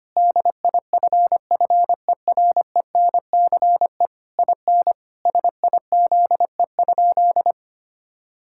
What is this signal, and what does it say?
Morse code, heard as a keyed tone, about 25 wpm, 700 Hz. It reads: DIFFERENCE IN SIZE?
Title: Sentences from Top 1000 Words - Review Part 5 - 25wpm